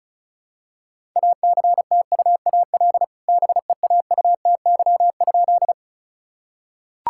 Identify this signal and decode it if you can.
Morse code, heard as a keyed tone, about 36 wpm, 700 Hz. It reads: ACTUAL BEAUTY?
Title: Sentences from Top 1000 Words - Review Part 5 - 35wpm